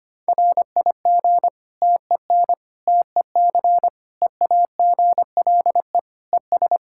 Morse code, heard as a keyed tone, about 25 wpm, 700 Hz. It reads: RIG TEN TEC EAGLE ES
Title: K1USN SST QSO Practice 25wpm